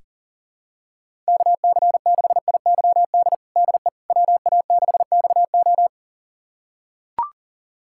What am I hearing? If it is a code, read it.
Morse code, heard as a keyed tone, about 40 wpm, 700 Hz. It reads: KC6IYD DE WA6XO
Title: USA Callsign DE Callsign 40wpm